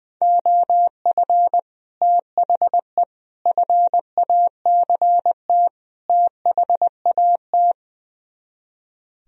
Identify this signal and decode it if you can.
Morse code, heard as a keyed tone, about 20 wpm, 700 Hz. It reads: OF THE FACT THAT
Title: Speed-Racing - Sets of 4 Words 20wpm